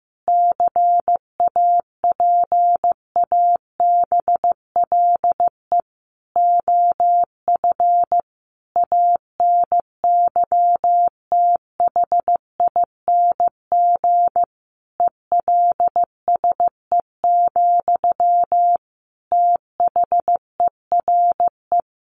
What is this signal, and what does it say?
Morse code, heard as a keyed tone, about 15 wpm, 700 Hz. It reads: CAPABLE OF ANYTHING ELSE, THERE
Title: Sentences from Top 1200 Words 15wpm